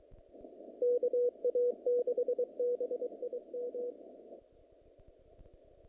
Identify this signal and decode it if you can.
Morse code, heard as a keyed tone, about 23 wpm, 470 Hz. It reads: KA6BIM
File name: contests / naqp-cw-k5tr-op-k5pi- -2200